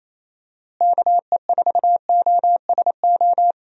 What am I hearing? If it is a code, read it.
Morse code, heard as a keyed tone, about 28 wpm, 700 Hz. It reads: KE4OSO
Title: USA Call Signs - 6 Characters - Repeated 3x - 28wpm